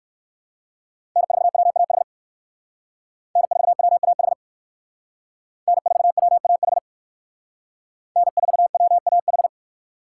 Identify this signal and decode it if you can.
Morse code, heard as a keyed tone, about 58 wpm, 700 Hz. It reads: N4WAH N4WAH N4WAH N4WAH